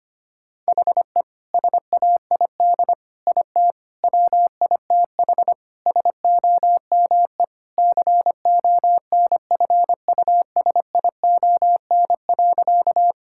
Morse code, heard as a keyed tone, about 25 wpm, 700 Hz. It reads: HE SAID IT WITH SOME CONFUSION.
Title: Sentences from Top 1600 Words 25wpm